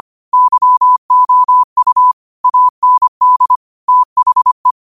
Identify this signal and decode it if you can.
Morse code, heard as a keyed tone, about 25 wpm, 1000 Hz. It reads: YOU AND THE
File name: qotdfiles / 25wpm